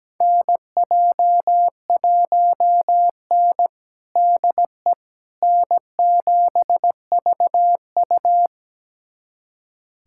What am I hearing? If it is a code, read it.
Morse code, heard as a keyed tone, about 17 wpm, 700 Hz. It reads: NJ1N DE N7VU
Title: USA Callsign DE Callsign - 4 Characters - 17wpm